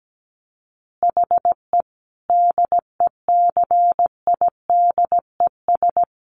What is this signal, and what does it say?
Morse code, heard as a keyed tone, about 17 wpm, 700 Hz. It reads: HE DECIDES
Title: Easy Phrases 17wpm